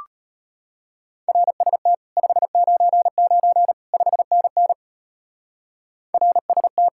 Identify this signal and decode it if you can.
Morse code, heard as a keyed tone, about 32 wpm, 700 Hz. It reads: RST 599 5NN RST